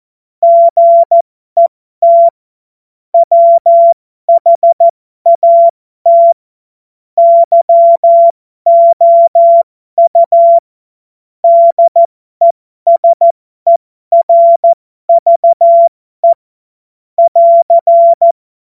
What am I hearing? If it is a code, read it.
Morse code, heard as a keyed tone, about 14 wpm, spaced as for 12 wpm, 680 Hz. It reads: GET WHAT YOU DESERVE <AR>
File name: Morsekurs / Witze